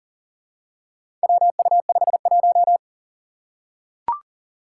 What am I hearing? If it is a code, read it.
Morse code, heard as a keyed tone, about 40 wpm, 700 Hz. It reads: WU51
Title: Four Challenging Letter-Number Combinations 40wpm